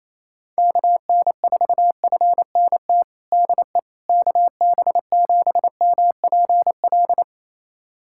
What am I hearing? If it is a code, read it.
Morse code, heard as a keyed tone, about 27 wpm, 700 Hz. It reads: KN4FNT DE KB7MPL
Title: USA Callsign DE Callsign - 6 Characters - 28wpm